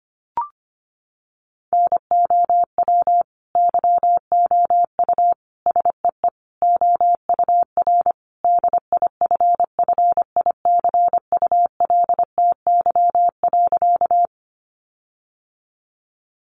Morse code, heard as a keyed tone, about 25 wpm, 700 Hz. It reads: NOW YOU SEE OUR DIFFICULTY.
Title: Sentences from Top 1500 Words 25wpm